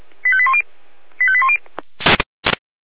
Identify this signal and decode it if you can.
Five-tone selective call (ZVEI), heard as quick runs of five tones, tones of about 70 ms. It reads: 85810 85810